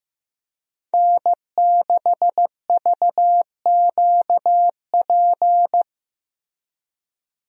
Morse code, heard as a keyed tone, about 15 wpm, 700 Hz. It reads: N6VQP
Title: USA Call Signs - 5 Characters - Repeated 3x - 15wpm